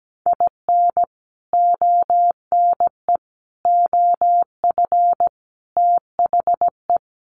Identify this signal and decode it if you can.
Morse code, heard as a keyed tone, about 17 wpm, 700 Hz. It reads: IN ONE OF THE
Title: Sets of 5 Words Encore 17wpm